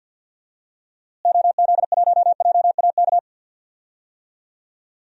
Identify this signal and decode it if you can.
Morse code, heard as a keyed tone, about 45 wpm, 700 Hz. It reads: OZ1JAK
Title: DXCC-100 Call Signs - 6 Characters - Repeated 2x - 50wpm